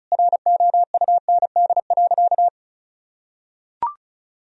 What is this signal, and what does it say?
Morse code, heard as a keyed tone, about 34 wpm, 700 Hz. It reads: ROUND.
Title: Sentences from Top 1000 Words - Review Part 1 - 35wpm